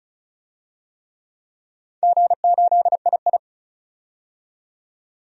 Morse code, heard as a keyed tone, about 36 wpm, 700 Hz. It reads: G8II
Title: DXCC-100 Call Signs - 4 Characters - 35wpm